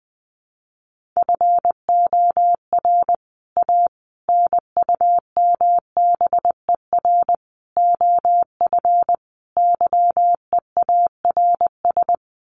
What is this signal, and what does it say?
Morse code, heard as a keyed tone, about 20 wpm, 700 Hz. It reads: FOR A NUMBER OF YEARS
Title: Sets of 5 Words Encore 20wpm